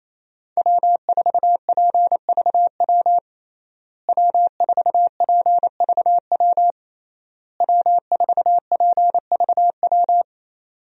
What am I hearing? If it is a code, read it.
Morse code, heard as a keyed tone, about 28 wpm, 700 Hz. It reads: W4PVW W4PVW W4PVW
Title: USA Call Signs - 5 Characters - Repeated 3x - 28wpm